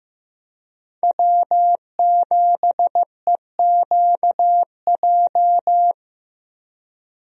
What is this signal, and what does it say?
Morse code, heard as a keyed tone, about 15 wpm, 700 Hz. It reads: W7EQJ
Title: USA Call Signs - 5 Characters - Repeated 2x - 15wpm